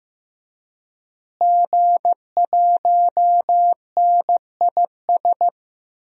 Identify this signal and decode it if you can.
Morse code, heard as a keyed tone, about 15 wpm, 700 Hz. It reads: G1NIS
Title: DXCC-100 Call Signs - 5 Characters - Repeated 3x - 15wpm